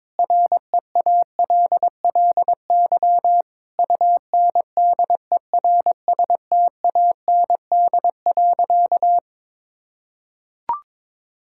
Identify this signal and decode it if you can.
Morse code, heard as a keyed tone, about 22 wpm, 700 Hz. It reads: REALLY UNDERSTAND.
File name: practice / 22wpm / Sentences with 4-Word Phrases - 22wpm